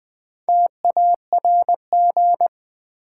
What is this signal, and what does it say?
Morse code, heard as a keyed tone, about 20 wpm, 700 Hz. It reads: TARG
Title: Build A Word - 2 Letters At A Time - 6-Letter Words - 20wpm